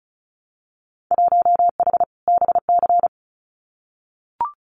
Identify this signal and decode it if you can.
Morse code, heard as a keyed tone, about 36 wpm, 700 Hz. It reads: 1H BC
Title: Winter Field Day Exchanges Encore 35wpm